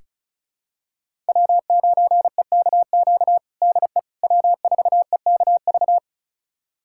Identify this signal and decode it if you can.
Morse code, heard as a keyed tone, about 34 wpm, 700 Hz. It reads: W9EKQ DE W4EKV
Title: USA Callsign DE Callsign - 5 Characters - 35wpm